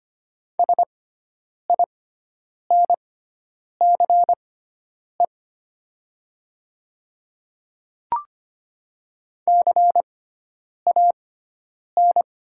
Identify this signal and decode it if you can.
Morse code, heard as a keyed tone, about 25 wpm, spaced as for 9 wpm, 700 Hz. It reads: SINCE CAN
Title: 6x Character Spacing - Top 300 Words - 25wpm